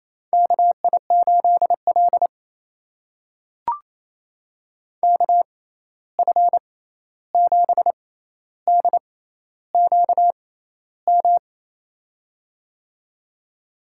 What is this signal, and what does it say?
Morse code, heard as a keyed tone, about 28 wpm, spaced as for 13 wpm, 700 Hz. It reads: KI8L KF7DQM
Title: USA Call Signs - 6x Character Spacing - 28wpm